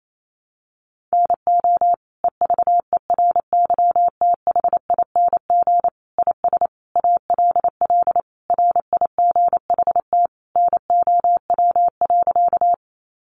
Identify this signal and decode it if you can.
Morse code, heard as a keyed tone, about 28 wpm, 700 Hz. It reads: NO EVERYTHING IS ALL RIGHT NOW.
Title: Sentences from Top 700 Words 28wpm